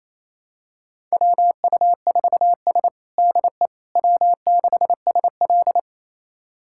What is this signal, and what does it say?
Morse code, heard as a keyed tone, about 28 wpm, 700 Hz. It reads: WU4S DE W6SL
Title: USA Callsign DE Callsign - 4 Characters - 28wpm